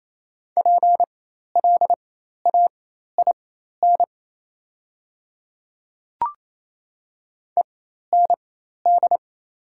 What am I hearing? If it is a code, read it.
Morse code, heard as a keyed tone, about 28 wpm, spaced as for 13 wpm, 700 Hz. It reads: PLAIN END
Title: 4x Character Spacing - Top 500 Words - 28wpm